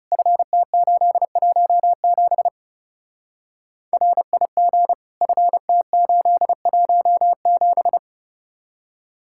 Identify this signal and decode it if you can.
Morse code, heard as a keyed tone, about 36 wpm, 700 Hz. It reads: FT817 RIG FT817
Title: Speed-Racing Common QSO Phrases Encore - 30wpm